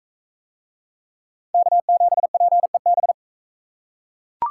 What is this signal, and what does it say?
Morse code, heard as a keyed tone, about 40 wpm, 700 Hz. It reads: K7PEB